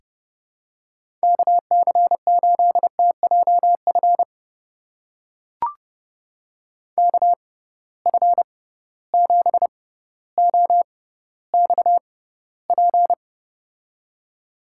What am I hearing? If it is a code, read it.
Morse code, heard as a keyed tone, about 31 wpm, spaced as for 15 wpm, 700 Hz. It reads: KC8TJF KF7OXP
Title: USA Call Signs - 6 Characters - 6x Character Spacing - 30wpm